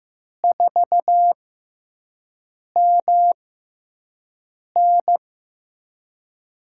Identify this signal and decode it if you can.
Morse code, heard as a keyed tone, about 15 wpm, spaced as for 5 wpm, 700 Hz. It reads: 4MN
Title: USA Call Signs - 5 Characters - 6x Character Spacing - 15wpm